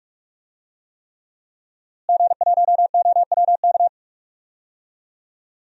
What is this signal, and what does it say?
Morse code, heard as a keyed tone, about 42 wpm, 700 Hz. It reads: G1OWK
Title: DXCC-100 Call Signs - 5 Characters - Repeated 3x - 45wpm